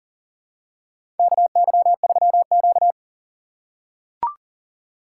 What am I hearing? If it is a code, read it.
Morse code, heard as a keyed tone, about 40 wpm, 700 Hz. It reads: KY3Q
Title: USA Call Signs - 4 Characters - 8x Character Spacing - 40wpm